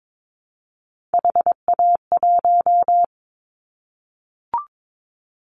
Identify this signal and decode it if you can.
Morse code, heard as a keyed tone, about 22 wpm, 700 Hz. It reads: HA1
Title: Callsign Prefixes for Top 25 DXCC - Repeated 3x - 22wpm